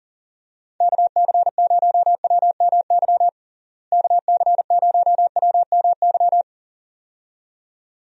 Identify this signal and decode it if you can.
Morse code, heard as a keyed tone, about 40 wpm, 700 Hz. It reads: KC0WMY KC0WMY
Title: USA Call Signs - 6 Characters - Repeated 2x - 40wpm